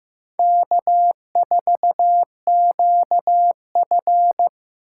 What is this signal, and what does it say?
Morse code, heard as a keyed tone, about 15 wpm, 700 Hz. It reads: K4QF
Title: USA Call Signs - 6x Character Spacing - 15wpm